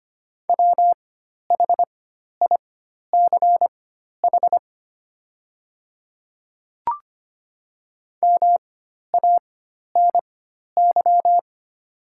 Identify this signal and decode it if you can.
Morse code, heard as a keyed tone, about 25 wpm, spaced as for 12 wpm, 690 Hz. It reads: WHICH MANY
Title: 4x Character Spacing - Top 100 Words - 25wpm